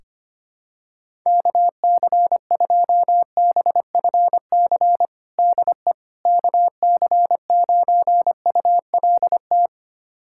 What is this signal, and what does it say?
Morse code, heard as a keyed tone, about 25 wpm, 700 Hz. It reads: KC2BFC DE KC9ULT
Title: USA Callsign DE Callsign - 6 Characters - 25wpm